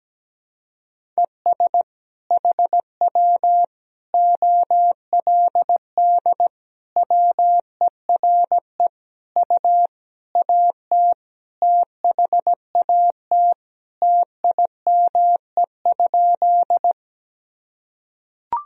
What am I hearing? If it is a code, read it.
Morse code, heard as a keyed tone, about 17 wpm, 700 Hz. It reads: ES HW OLD WERE U AT THAT TIME?
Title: Sentences from Top 200 Words and Abbreviations 17wpm